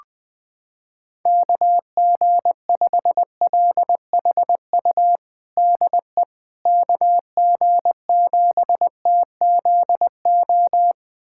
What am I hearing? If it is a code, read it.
Morse code, heard as a keyed tone, about 20 wpm, 700 Hz. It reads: KG5LHU DE KG7TZO
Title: USA Callsign DE Callsign - 6 Characters - 20wpm